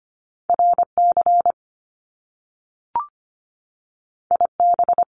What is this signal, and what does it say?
Morse code, heard as a keyed tone, about 25 wpm, 700 Hz. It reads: RC IB
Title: Top Two Letter Combinations in Words 25wpm